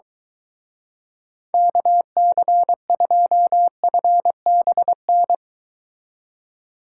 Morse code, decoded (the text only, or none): KC2FBN